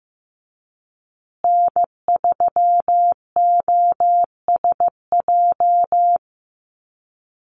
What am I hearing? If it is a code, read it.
Morse code, heard as a keyed tone, about 15 wpm, 700 Hz. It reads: N3OSJ
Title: USA Call Signs Repeated 2x 15wpm